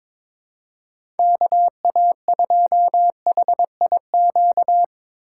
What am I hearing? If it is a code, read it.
Morse code, heard as a keyed tone, about 22 wpm, 700 Hz. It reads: KA2HIQ
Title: USA Call Signs - 6 Characters - 8x Character Spacing - 22wpm